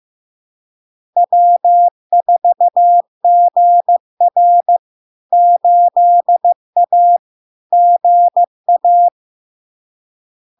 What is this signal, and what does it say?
Morse code, heard as a keyed tone, about 15 wpm, 700 Hz. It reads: W4GR 8A GA
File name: LARK-FD-2022-Log-in-CW-15wpm-700-Hz